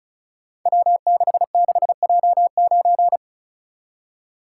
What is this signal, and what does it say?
Morse code, heard as a keyed tone, about 36 wpm, 700 Hz. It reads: W66J9
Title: Code Groups - 5 Letter:Numbers - 35wpm